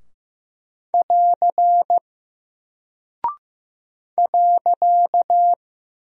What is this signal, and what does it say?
Morse code, heard as a keyed tone, about 15 wpm, 700 Hz. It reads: <AR> .
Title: Single Character (Letter:Number:Punctuation:Prosign) Alt 15wpm